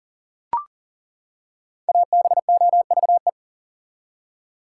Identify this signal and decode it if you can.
Morse code, heard as a keyed tone, about 40 wpm, 700 Hz. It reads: ABOVE
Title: Five Letter Words - 40wpm